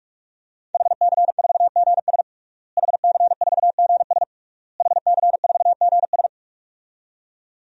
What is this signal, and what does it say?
Morse code, heard as a keyed tone, about 46 wpm, 700 Hz. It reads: HC4GS HC4GS HC4GS
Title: DXCC-100 Call Signs - 5 Characters - Repeated 3x - 45wpm